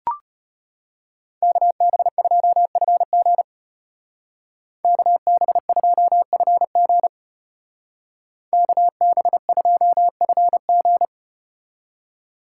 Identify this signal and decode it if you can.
Morse code, heard as a keyed tone, about 33 wpm, 700 Hz. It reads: KB2FG KB2FG KB2FG